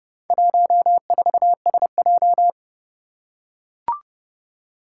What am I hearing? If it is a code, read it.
Morse code, heard as a keyed tone, about 28 wpm, 700 Hz. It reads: 14SJ
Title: Four Challenging Letter-Number Combinations 30wpm